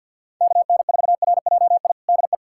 Morse code, heard as a keyed tone, about 52 wpm, 700 Hz. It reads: KN4RJI DE
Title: USA Callsign DE Callsign - 6 Characters - 50wpm